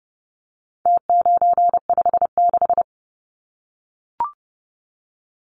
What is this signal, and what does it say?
Morse code, heard as a keyed tone, about 30 wpm, 700 Hz. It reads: T956
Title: Numbers - Contest Style - 4-Digit 30wpm